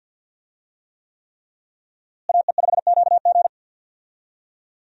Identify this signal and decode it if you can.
Morse code, heard as a keyed tone, about 53 wpm, 700 Hz. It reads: AE5QG
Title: USA Call Signs - 5 Characters - Repeated 3x - 50wpm